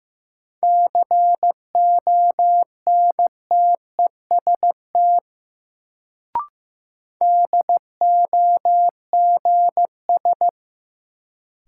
Morse code, heard as a keyed tone, about 15 wpm, 700 Hz. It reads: CONTEST DOGS
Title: Sentences from Top 1600 Words - New Word Drill - 15wpm